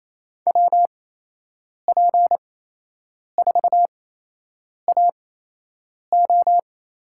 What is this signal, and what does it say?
Morse code, heard as a keyed tone, about 28 wpm, spaced as for 8 wpm, 700 Hz. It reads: WP4AO
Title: USA Call Signs - 8x Character Spacing - 28wpm